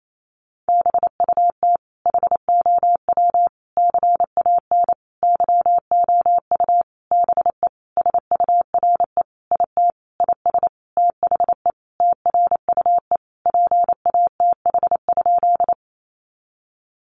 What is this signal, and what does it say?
Morse code, heard as a keyed tone, about 28 wpm, 700 Hz. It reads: BUT HOW CAN YOU BE SURE IT IS THE TRUE PATH?
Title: Sentences from Top 1400 Words 28wpm